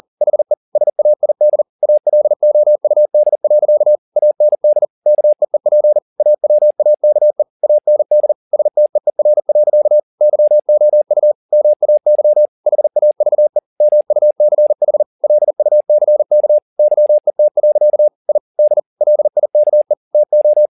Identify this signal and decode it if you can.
Morse code, heard as a keyed tone, about 40 wpm, 600 Hz. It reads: HE SAID ALOUD. AND KEEP AWAKE AND STEER. YOU MAY HAVE MUCH LUCK YET. I D LIKE TO
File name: the old man and the sea 2 40wpm